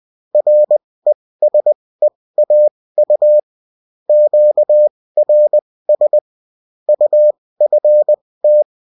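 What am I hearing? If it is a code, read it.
Morse code, heard as a keyed tone, about 20 wpm, spaced as for 16 wpm, 600 Hz. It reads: RESEAU QRS UFT
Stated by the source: Morse code bulletin of a French amateur CW union, sent as text and in CW.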